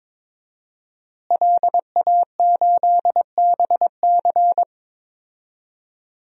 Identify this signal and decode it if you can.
Morse code, heard as a keyed tone, about 22 wpm, 700 Hz. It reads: LA8BC
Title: DXCC-100 Call Signs - 5 Characters - Repeated 2x - 22wpm